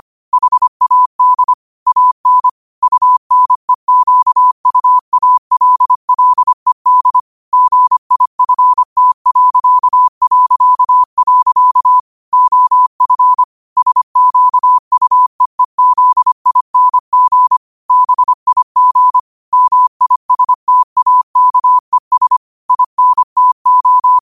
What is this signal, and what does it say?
Morse code, heard as a keyed tone, about 25 wpm, 1000 Hz. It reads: HAD AN UNEQUALLED GIFT... OF SQUEEZING BIG MISTAKES INTO
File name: qotdfiles / 25wpm